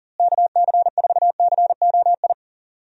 Morse code, heard as a keyed tone, about 40 wpm, 700 Hz. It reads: KC4COI
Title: USA Call Signs - 6 Characters - Repeated 2x - 40wpm